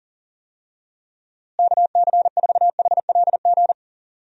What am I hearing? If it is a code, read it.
Morse code, heard as a keyed tone, about 40 wpm, 700 Hz. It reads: KC4HLG
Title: USA Call Signs - 6 Characters - Repeated 3x - 40wpm